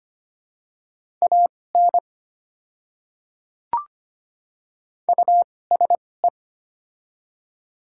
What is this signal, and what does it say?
Morse code, heard as a keyed tone, about 25 wpm, spaced as for 18 wpm, 690 Hz. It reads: AN USE
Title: 2x Character Spacing - Top 100 Words - 25wpm